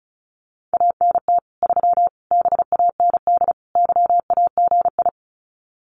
Morse code, heard as a keyed tone, about 34 wpm, 700 Hz. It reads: ANT 3 BAND YAGI
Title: Common QSO Phrases 35wpm